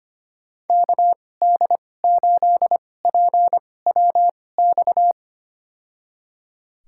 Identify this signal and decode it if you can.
Morse code, heard as a keyed tone, about 25 wpm, spaced as for 18 wpm, 700 Hz. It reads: KD8PWX
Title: USA Call Signs - 6 Characters - 2x Character Spacing - 25wpm